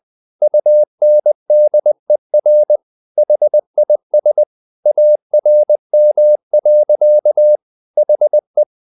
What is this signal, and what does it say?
Morse code, heard as a keyed tone, about 20 wpm, 600 Hz. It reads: UNDER HIS ARM. HE
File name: the old man and the sea part 2 20wpm